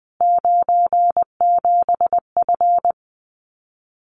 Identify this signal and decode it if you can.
Morse code, heard as a keyed tone, about 20 wpm, 700 Hz. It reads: 97F
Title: Common QSO Phrases Encore 20wpm